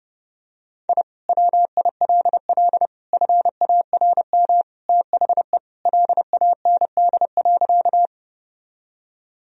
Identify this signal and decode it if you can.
Morse code, heard as a keyed tone, about 30 wpm, 700 Hz. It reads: I WILL FARM THE LAND.
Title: Sentences from Top 1000 Words - Review Part 2 - 30wpm